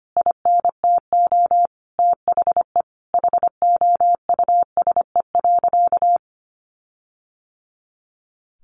Morse code, heard as a keyed tone, about 25 wpm, 700 Hz. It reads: INTO THE HOUSE.
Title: Sentences from Top 1500 Words 25wpm